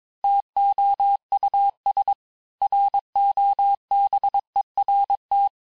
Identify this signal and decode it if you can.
Morse code, heard as a keyed tone, about 22 wpm, 800 Hz. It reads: TOUS ROBERT
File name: cours-du-mercredi / audios / cours-854-liste-fin-QSO-vitesse-21-mpm